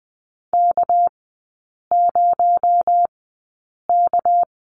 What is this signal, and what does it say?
Morse code, heard as a keyed tone, about 20 wpm, spaced as for 8 wpm, 700 Hz. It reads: K0K
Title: 218 Lesson 35e - Number 0 - Three Characters 20wpm